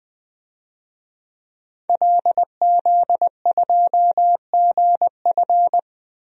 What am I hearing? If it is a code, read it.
Morse code, heard as a keyed tone, about 20 wpm, 700 Hz. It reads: LZ2GF
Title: DXCC-100 Call Signs - 5 Characters - 20wpm